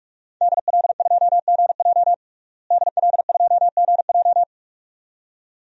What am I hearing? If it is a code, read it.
Morse code, heard as a keyed tone, about 46 wpm, 700 Hz. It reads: DL2GJ DL2GJ
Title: DXCC-100 Call Signs - 5 Characters - Repeated 2x - 45wpm